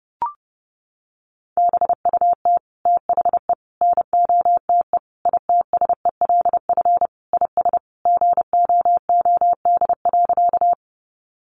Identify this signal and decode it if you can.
Morse code, heard as a keyed tone, about 30 wpm, 700 Hz. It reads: BUT THE NOTE ITSELF IS GOOD.
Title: Sentences from Top 800 Words 30wpm